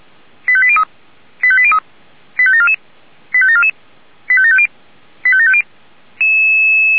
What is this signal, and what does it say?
Five-tone selective call (ZVEI), heard as quick runs of five tones, tones of about 70 ms. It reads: 85803 85803 85744 85744 85750 85750